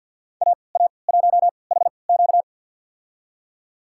Morse code, heard as a keyed tone, about 53 wpm, spaced as for 29 wpm, 700 Hz. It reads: AA1HQ
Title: USA Call Signs - 5 Characters - 3x Character Spacing - 50wpm